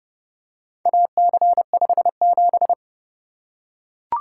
Morse code, heard as a keyed tone, about 30 wpm, 700 Hz. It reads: AC57